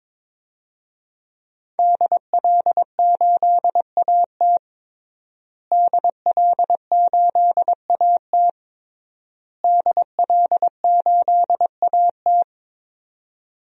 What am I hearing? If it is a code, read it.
Morse code, heard as a keyed tone, about 22 wpm, 700 Hz. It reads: DL8AT DL8AT DL8AT
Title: DXCC-100 Call Signs - 5 Characters - Repeated 3x - 22wpm